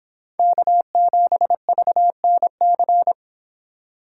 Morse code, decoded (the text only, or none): K7VNC